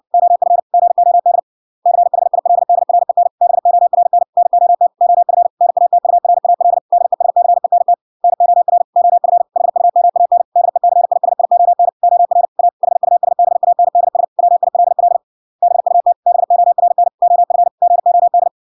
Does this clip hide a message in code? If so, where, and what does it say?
Yes: Morse code from 0.1 to 18.5 s, heard as a keyed tone, about 56 wpm, 700 Hz. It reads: OF GOD CHILDREN BORN NOT OF NATURAL DESCENT NOR OF HUMAN DECISION OR A HUSBANDS WILL BUT BORN OF GOD